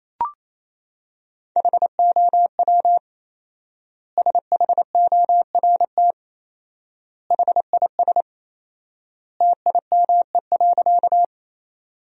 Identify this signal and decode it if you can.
Morse code, heard as a keyed tone, about 27 wpm, 700 Hz. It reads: HOW SHORT HIS TIME.
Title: 4x Word-Spacing - Sentences from Top 400 Words - 28wpm